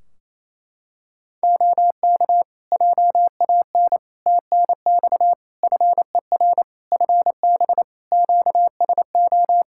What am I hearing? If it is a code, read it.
Morse code, heard as a keyed tone, about 28 wpm, 700 Hz. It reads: OK JAN TNX FER FB QSO
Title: K1USN SST QSO Practice 28wpm